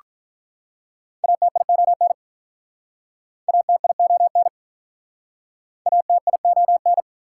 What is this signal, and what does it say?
Morse code, heard as a keyed tone, about 47 wpm, 700 Hz. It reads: ATION ATION ATION